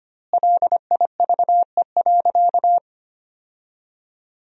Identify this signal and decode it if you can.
Morse code, heard as a keyed tone, about 25 wpm, 700 Hz. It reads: LIVE.
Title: 2x Word-Spacing - Sentences from Top 300 Words - 25wpm